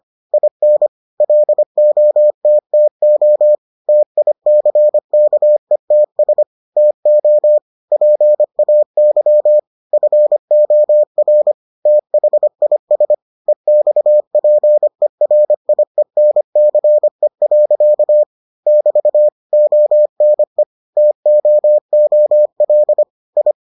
Morse code, heard as a keyed tone, about 25 wpm, 600 Hz. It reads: IN LOTTO TICKETS TO PAY FOR THIS EXPERIENCE. <BT> ONE TOOL I